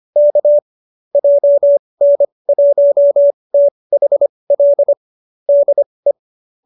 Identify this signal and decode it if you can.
Morse code, heard as a keyed tone, about 25 wpm, spaced as for 20 wpm, 590 Hz. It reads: K JN1THL DE